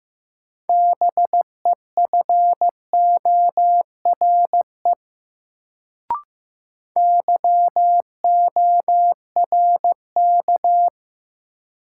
Morse code, heard as a keyed tone, about 15 wpm, 700 Hz. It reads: BEFORE YORK
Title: Sets of 2 Words - New Word Drill - 15wpm